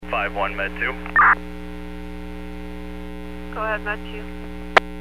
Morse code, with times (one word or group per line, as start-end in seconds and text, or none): none